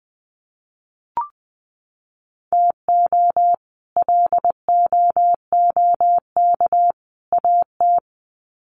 Morse code, heard as a keyed tone, about 20 wpm, 700 Hz. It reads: TO LOOK AT
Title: Sets of 3 Words 20wpm